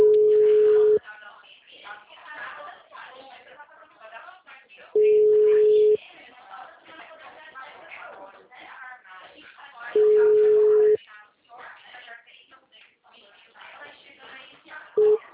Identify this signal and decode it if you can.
Ringback tone (Europe, 425 Hz), with speech.
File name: out- -2033- - - .84485